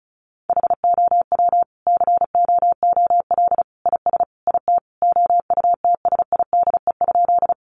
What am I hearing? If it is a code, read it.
Morse code, heard as a keyed tone, about 34 wpm, 700 Hz. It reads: HOW COOL IS IT OUTSIDE?
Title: Sentences from Top 1200 Words 35wpm